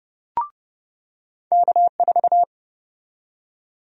Morse code, heard as a keyed tone, about 30 wpm, 700 Hz. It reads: K4
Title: USA Call Sign Prefixes - 30wpm